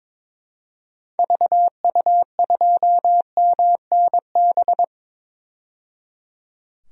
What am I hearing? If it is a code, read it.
Morse code, heard as a keyed tone, about 22 wpm, 700 Hz. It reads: VU2MNB
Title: DXCC-100 Call Signs - 6 Characters - 22wpm